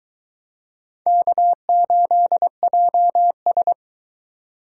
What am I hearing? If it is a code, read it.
Morse code, heard as a keyed tone, about 22 wpm, 700 Hz. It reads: K8JS